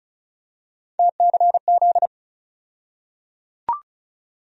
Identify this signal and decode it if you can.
Morse code, heard as a keyed tone, about 34 wpm, 700 Hz. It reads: TCZ
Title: USA Call Sign Postfixes - 3 Letters 35wpm